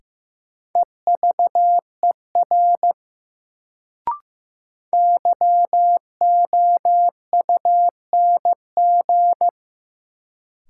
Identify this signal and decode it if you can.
Morse code, heard as a keyed tone, about 15 wpm, 700 Hz. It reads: EVER YOUNG
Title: Sentences from Top 400 Words - New Word Drill - 15wpm